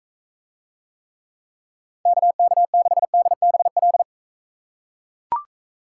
Morse code, heard as a keyed tone, about 43 wpm, 700 Hz. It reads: KK6DBL